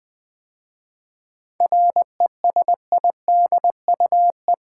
Morse code, heard as a keyed tone, about 20 wpm, 700 Hz. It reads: RESIDUE
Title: 7 Letter Words - First 3 Letters Given - 20wpm